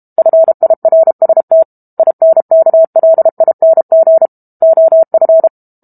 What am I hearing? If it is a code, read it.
Morse code, heard as a keyed tone, about 32 wpm, 650 Hz. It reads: FIRST INKLING OF
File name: Book Two / 06-Chapter 6 - 32 WPM